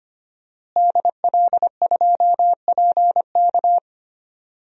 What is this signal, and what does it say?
Morse code, heard as a keyed tone, about 25 wpm, 700 Hz. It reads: DL2PK
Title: DXCC-100 Call Signs - 5 Characters - Repeated 3x - 25wpm